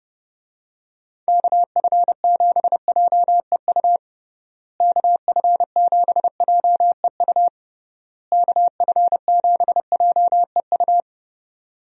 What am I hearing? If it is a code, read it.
Morse code, heard as a keyed tone, about 30 wpm, 700 Hz. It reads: KF7JEU KF7JEU KF7JEU
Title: USA Call Signs - 6 Characters - Repeated 3x - 30wpm